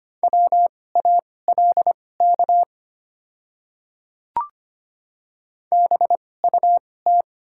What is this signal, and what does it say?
Morse code, heard as a keyed tone, about 25 wpm, spaced as for 18 wpm, 700 Hz. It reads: WALK BUT
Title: 2x Character Spacing - Top 300 Words - 25wpm